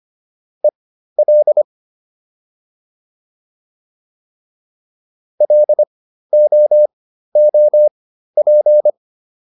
Morse code, heard as a keyed tone, about 25 wpm, spaced as for 13 wpm, 600 Hz. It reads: EL LOOP